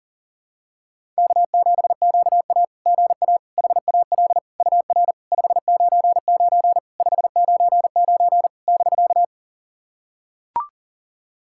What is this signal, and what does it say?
Morse code, heard as a keyed tone, about 37 wpm, 690 Hz. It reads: K7QA GA HAL UR 599 599 <BK>
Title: SOTA QSO Practice 40wpm